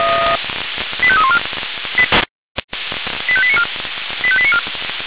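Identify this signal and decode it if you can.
Five-tone selective call (ZVEI), heard as quick runs of five tones, tones of about 70 ms. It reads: 85314 85894 85894
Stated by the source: Received on SDR as a ZVEI five-tone selective call, code 85320.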